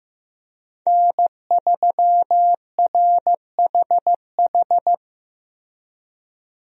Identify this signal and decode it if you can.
Morse code, heard as a keyed tone, about 15 wpm, 700 Hz. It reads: N3RHH
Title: USA Call Signs - 5 Characters - Repeated 2x - 15wpm